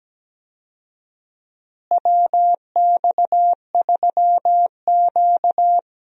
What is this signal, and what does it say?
Morse code, heard as a keyed tone, about 17 wpm, 700 Hz. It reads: WX3Q
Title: USA Call Signs - 4 Characters - Repeated 2x - 17wpm